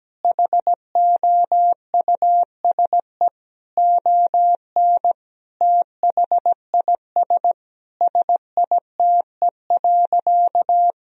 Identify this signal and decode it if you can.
Morse code, heard as a keyed tone, about 17 wpm, 700 Hz. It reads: HOUSE ON THIS SITE.
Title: Sentences from Top 1000 Words - Review Part 4 - 17wpm